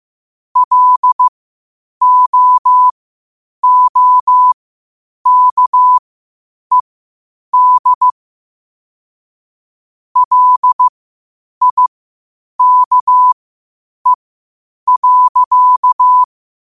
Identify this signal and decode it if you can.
Morse code, heard as a keyed tone, about 15 wpm, spaced as for 8 wpm, 1000 Hz. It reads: LOOKED LIKE.
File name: qotdfiles / 8wpm